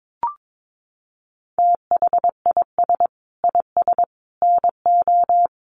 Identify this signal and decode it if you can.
Morse code, heard as a keyed tone, about 22 wpm, 700 Hz. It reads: THIS IS NO
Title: Sentences from Top 700 Words 22wpm